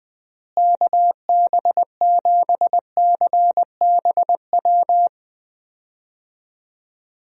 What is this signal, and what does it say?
Morse code, heard as a keyed tone, about 20 wpm, 700 Hz. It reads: KB7CBW
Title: USA Call Signs - 6 Characters - Repeated 3x - 20wpm